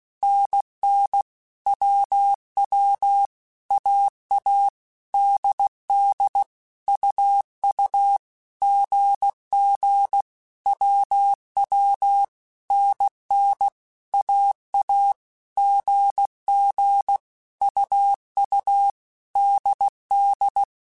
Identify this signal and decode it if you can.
Morse code, heard as a keyed tone, about 16 wpm, 800 Hz. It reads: NN WW AA DD UU GG WW NN AA GG UU DD
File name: Basic Letter Groups By7Groups / AWUNDG / AWUNDG@15